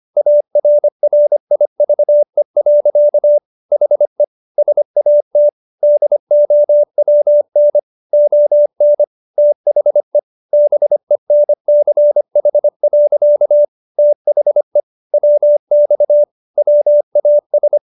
Morse code, heard as a keyed tone, about 25 wpm, 590 Hz. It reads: ARRIVE. HE SAT DOWN ON THE BENCH. THE WX WAS